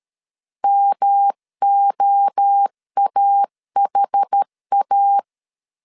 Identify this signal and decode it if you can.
Morse code, heard as a keyed tone, about 13 wpm, 800 Hz. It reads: MOAHA
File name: CW / DET / 1 3 1500C S A 12wpm 2905